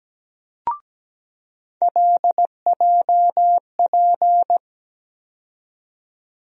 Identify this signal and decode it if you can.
Morse code, heard as a keyed tone, about 17 wpm, 700 Hz. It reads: LJP
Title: USA Call Sign Postfixes - 3 Letters 17wpm